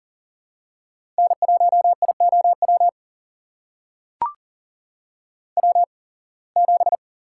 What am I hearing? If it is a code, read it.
Morse code, heard as a keyed tone, about 39 wpm, spaced as for 26 wpm, 700 Hz. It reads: N1IOW W7
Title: USA Call Signs - 5 Characters - 8x Character Spacing - 40wpm